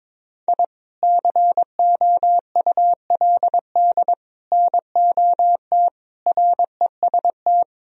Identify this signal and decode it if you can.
Morse code, heard as a keyed tone, about 22 wpm, 700 Hz. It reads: I COULD NOT REST
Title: Sentences from Top 1000 Words - Review Part 1 - 22wpm